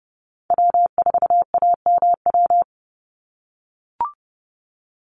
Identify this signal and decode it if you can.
Morse code, heard as a keyed tone, about 30 wpm, 700 Hz. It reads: W4AMW
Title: NA SOTA Chaser Callsigns 30wpm